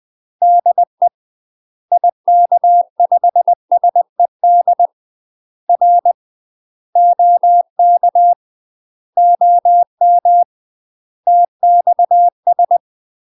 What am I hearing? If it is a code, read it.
Morse code, heard as a keyed tone, about 20 wpm, 700 Hz. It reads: DE IK5SED R OK OM TXS
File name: UK MASTER QSOs TRAINING / 20WPM / UK MASTER QSOs 0052